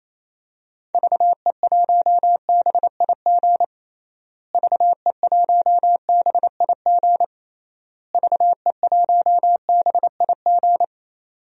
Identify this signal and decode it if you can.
Morse code, heard as a keyed tone, about 28 wpm, 700 Hz. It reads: VE1BIG VE1BIG VE1BIG
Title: DXCC-100 Call Signs - 6 Characters - Repeated 3x - 28wpm